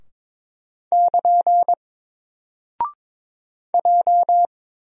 Morse code, heard as a keyed tone, about 22 wpm, 700 Hz. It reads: <KN> J
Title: Single Character (Letter:Number:Punctuation:Prosign) 22wpm